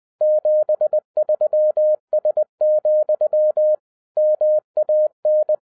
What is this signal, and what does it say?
Morse code, heard as a keyed tone, about 20 wpm, 610 Hz. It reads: 73S, MAN
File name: 2012-01-mp3 / n16-THROUGH EIGHT MONTHS-20wpm